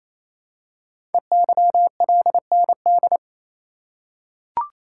Morse code, heard as a keyed tone, about 27 wpm, 700 Hz. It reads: EYLND
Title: Code Groups - 5 Letters - 28wpm